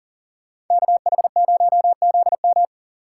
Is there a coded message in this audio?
Yes: Morse code from 0.7 to 2.6 s, heard as a keyed tone, about 38 wpm, 700 Hz. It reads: KH0ZM